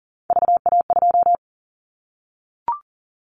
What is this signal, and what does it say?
Morse code, heard as a keyed tone, about 40 wpm, 700 Hz. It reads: VA2
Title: Callsign Prefixes for Top 25 DXCC - Repeated 3x - 40wpm